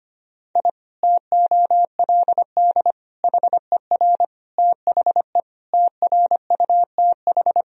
Morse code, heard as a keyed tone, about 25 wpm, 700 Hz. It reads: I TOLD HER THE TRUTH
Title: Sentences from Top 1000 Words - Review Part 3 - 25wpm